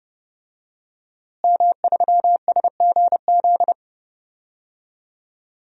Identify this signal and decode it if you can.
Morse code, heard as a keyed tone, about 31 wpm, 700 Hz. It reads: M3SGZ
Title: DXCC-100 Call Signs - 5 Characters - Repeated 2x - 30wpm